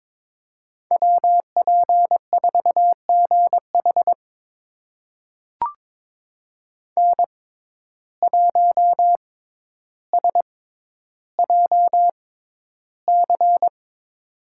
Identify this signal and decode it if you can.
Morse code, heard as a keyed tone, about 22 wpm, spaced as for 11 wpm, 700 Hz. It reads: WP4GH N1SJC
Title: USA Call Signs - 5 Characters - 6x Character Spacing - 22wpm